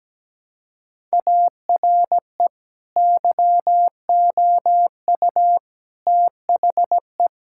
Morse code, heard as a keyed tone, about 17 wpm, 700 Hz. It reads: ARE YOU THE
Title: Sentences from Top 1600 Words 17wpm